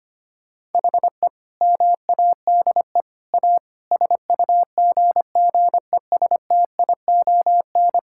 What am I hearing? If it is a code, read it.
Morse code, heard as a keyed tone, about 25 wpm, 700 Hz. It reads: HE MADE A SUGGESTION
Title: Sentences from Top 1600 Words 25wpm